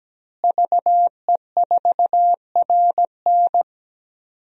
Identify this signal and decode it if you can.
Morse code, heard as a keyed tone, about 17 wpm, 700 Hz. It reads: VE4RN
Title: DXCC-100 Call Signs - 5 Characters - Repeated 3x - 17wpm